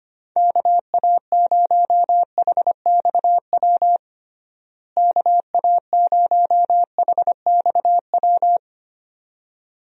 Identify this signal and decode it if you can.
Morse code, heard as a keyed tone, about 25 wpm, 700 Hz. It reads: KA0HXW KA0HXW
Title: USA Call Signs - 6 Characters - Repeated 2x - 25wpm